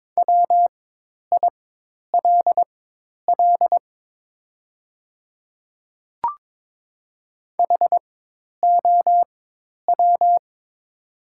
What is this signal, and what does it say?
Morse code, heard as a keyed tone, about 22 wpm, spaced as for 10 wpm, 700 Hz. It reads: WILL HOW
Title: 4x Character Spacing - Top 100 Words - 22wpm